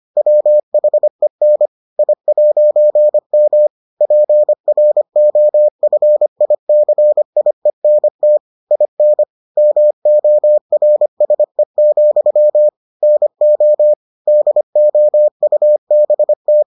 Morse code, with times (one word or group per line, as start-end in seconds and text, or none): 0.2-1.7 WHEN
2.0-3.7 I'M
4.0-8.4 PROFICIENT
8.7-9.2 IN
9.6-12.7 MORSE,
13.0-13.9 NO
14.3-16.6 DOUBT